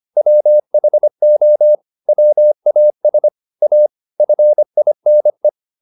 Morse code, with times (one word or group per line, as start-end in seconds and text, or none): 0.2-1.8 WHO
2.1-3.3 WAS
3.6-3.9 A
4.2-5.5 FINE